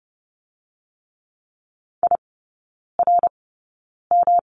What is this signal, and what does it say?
Morse code, heard as a keyed tone, about 30 wpm, spaced as for 9 wpm, 700 Hz. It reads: IRM
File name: Lesson 18e - Letter m - Three Characters 30wpm